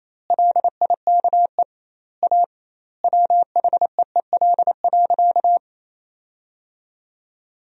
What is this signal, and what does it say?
Morse code, heard as a keyed tone, about 27 wpm, 700 Hz. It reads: LIKE A WHEEL.
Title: 2x Word-Spacing - Sentences from Top 500 Words - 28wpm